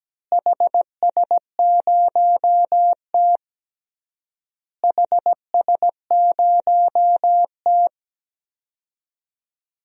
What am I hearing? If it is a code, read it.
Morse code, heard as a keyed tone, about 17 wpm, 700 Hz. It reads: HS0T HS0T
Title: DXCC-100 Call Signs - 4 Characters - Repeated 3x - 17wpm